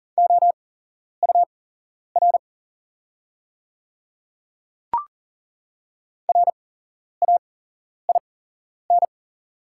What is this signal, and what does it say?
Morse code, heard as a keyed tone, about 40 wpm, spaced as for 11 wpm, 700 Hz. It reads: OUR RAIN
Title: 8x Character Spacing - Top 300 Words - 40wpm